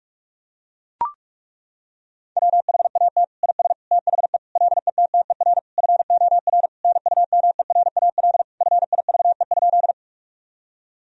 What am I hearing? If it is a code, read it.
Morse code, heard as a keyed tone, about 41 wpm, 700 Hz. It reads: WHAT IS THE LETTER FOR NUMERAL FIVE?
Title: Sentences from Top 1000 Words - Review Part 2 - 45wpm